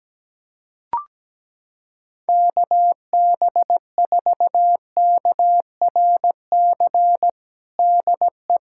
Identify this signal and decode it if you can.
Morse code, heard as a keyed tone, about 17 wpm, 700 Hz. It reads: KB4KRC DE
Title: USA Callsign DE Callsign - 6 Characters - 17wpm